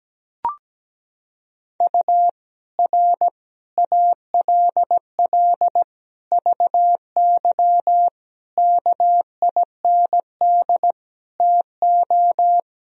Morse code, heard as a keyed tone, about 17 wpm, 700 Hz. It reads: U R ALL VY KIND TO
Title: Sentences from Top 200 Words and Abbreviations 17wpm